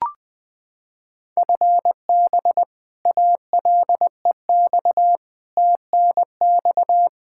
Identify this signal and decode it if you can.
Morse code, heard as a keyed tone, about 20 wpm, 700 Hz. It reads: FB ALEX TNX
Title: Practice QSO - Exchange 3 - More Info - 20wpm